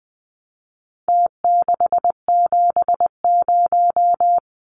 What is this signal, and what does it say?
Morse code, heard as a keyed tone, about 20 wpm, 700 Hz. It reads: T670
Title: Numbers - Contest Style - 4-Digit 20wpm